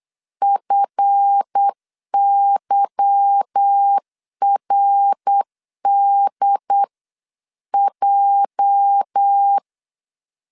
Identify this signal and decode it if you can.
Morse code, heard as a keyed tone, about 8 wpm, 800 Hz. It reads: FYRDJ
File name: CW / DET / 1 9 1500C S B 8wpm 5624